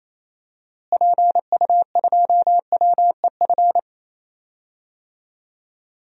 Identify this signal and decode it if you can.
Morse code, heard as a keyed tone, about 28 wpm, 700 Hz. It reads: PU2WEF
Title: DXCC-100 Call Signs - 6 Characters - Repeated 3x - 28wpm